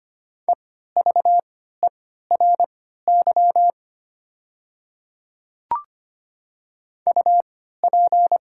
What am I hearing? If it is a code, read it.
Morse code, heard as a keyed tone, about 25 wpm, spaced as for 14 wpm, 690 Hz. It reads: EVERY UP
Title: 3x Character Spacing - Top 200 Words - 25wpm